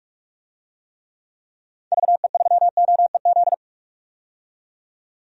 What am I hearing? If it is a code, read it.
Morse code, heard as a keyed tone, about 46 wpm, 700 Hz. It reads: VE3OEZ
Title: DXCC-100 Call Signs - 6 Characters - 45wpm